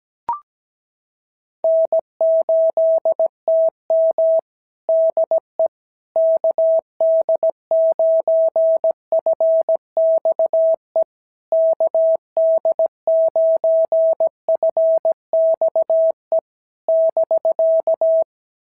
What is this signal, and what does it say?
Morse code, heard as a keyed tone, about 17 wpm, 650 Hz. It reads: N8TM DE KD9FXE KD9FXE <BK>
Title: SOTA QSO Practice 17wpm